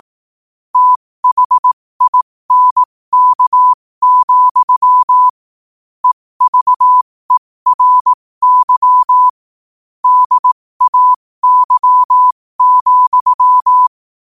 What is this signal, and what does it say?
Morse code, heard as a keyed tone, about 18 wpm, spaced as for 15 wpm, 1000 Hz. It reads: THINK, EVERY DAY,
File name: qotdfiles / 15wpm